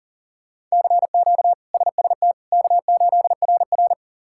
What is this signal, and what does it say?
Morse code, heard as a keyed tone, about 40 wpm, 700 Hz. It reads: CQ SST K8RR
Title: K1USN SST QSO Practice - 40wpm